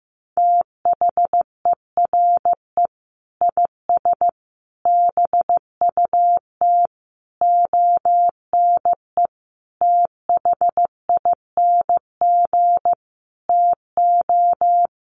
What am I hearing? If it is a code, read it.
Morse code, heard as a keyed tone, about 15 wpm, 700 Hz. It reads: THERE IS BUT ONE THING TO
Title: Sentences from Top 100 Words 15wpm